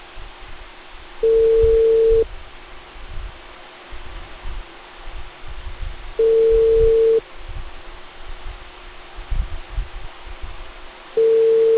Ringback tone (China, 450 Hz).